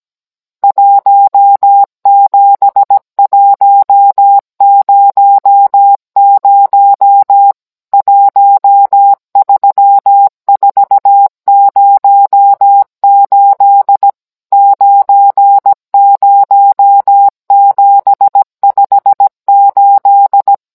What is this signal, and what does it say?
Morse code, heard as a keyed tone, about 17 wpm, 800 Hz. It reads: 17100 13408 90758